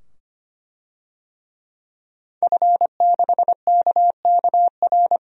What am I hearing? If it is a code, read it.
Morse code, heard as a keyed tone, about 25 wpm, 700 Hz. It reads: F6KKR
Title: DXCC-100 Call Signs - 5 Characters - 25wpm